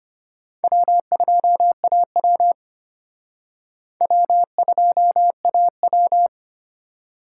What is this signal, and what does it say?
Morse code, heard as a keyed tone, about 31 wpm, 700 Hz. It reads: W2AW W2AW
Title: Speed Racing - USA 1x2 Call Signs 25wpm